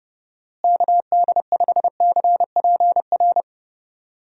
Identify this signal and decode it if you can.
Morse code, heard as a keyed tone, about 32 wpm, 700 Hz. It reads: KD5CPR
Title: USA Call Signs - 6 Characters - Repeated 2x - 30wpm